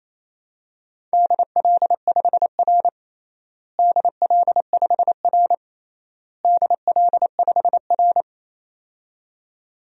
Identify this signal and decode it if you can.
Morse code, heard as a keyed tone, about 27 wpm, 700 Hz. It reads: DL5R DL5R DL5R
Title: DXCC-100 Call Signs - 4 Characters - Repeated 3x - 28wpm